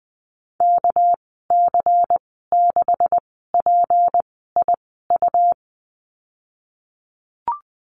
Morse code, heard as a keyed tone, about 20 wpm, spaced as for 14 wpm, 700 Hz. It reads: KC6PIU
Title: 238 Lesson 38g - Letter q - Call Signs 20wpm